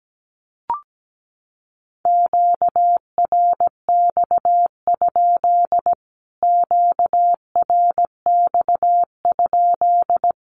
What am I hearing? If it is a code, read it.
Morse code, heard as a keyed tone, about 17 wpm, 700 Hz. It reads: QRX? QRX?
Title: Q-Codes Repeated 3x 17wpm